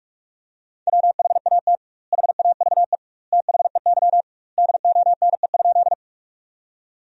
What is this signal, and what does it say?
Morse code, heard as a keyed tone, about 46 wpm, 700 Hz. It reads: WHAT HAVE THEY DONE?
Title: 2x Word-Spacing - Sentences from Top 500 Words - 45wpm